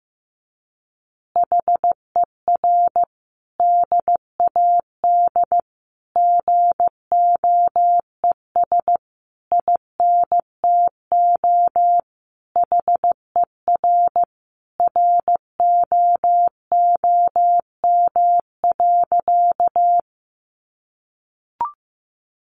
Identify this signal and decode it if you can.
Morse code, heard as a keyed tone, about 15 wpm, 700 Hz. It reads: HER DAD GOES INTO HER ROOM.
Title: Additional Easy Phrases 15wpm